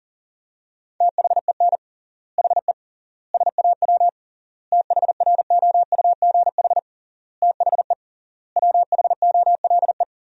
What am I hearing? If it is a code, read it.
Morse code, heard as a keyed tone, about 40 wpm, 700 Hz. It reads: THEN HE SAW THROUGH THE WHOLE
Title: 3x Word-Spacing - Sentences from Top 400 Words - 40wpm